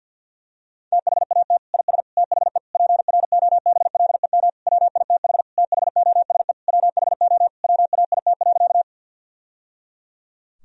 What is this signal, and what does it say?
Morse code, heard as a keyed tone, about 52 wpm, 700 Hz. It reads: THAT IS THE PROBLEM WITH THOSE WHO WAIT.
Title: Sentences from Top 1000 Words - Review Part 2 - 50wpm